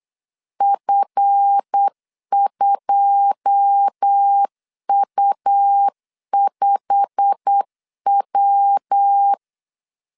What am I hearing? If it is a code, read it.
Morse code, heard as a keyed tone, about 8 wpm, 800 Hz. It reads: F2U5W
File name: CW / DET / 1 9 1500C S A 8wpm 5637